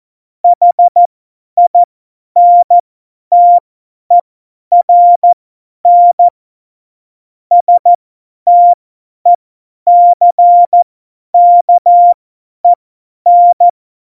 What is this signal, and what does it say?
Morse code, heard as a keyed tone, about 14 wpm, spaced as for 10 wpm, 700 Hz. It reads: HINTERN STECKEN
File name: Morsekurs / Witze / Witz 22. September 2020, 50 F70